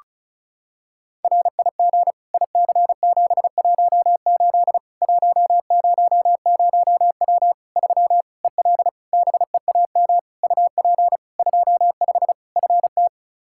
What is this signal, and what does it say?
Morse code, heard as a keyed tone, about 36 wpm, 700 Hz. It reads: RIG IC718 100W 3 EL BEAM UP 25 FT
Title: Practice QSO - Exchange 2 - Info - 35wpm